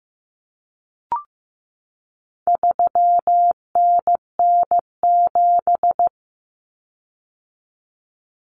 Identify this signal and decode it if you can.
Morse code, heard as a keyed tone, about 15 wpm, 700 Hz. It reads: 3NN7
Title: Common Cut Numbers - 4-Digit 15wpm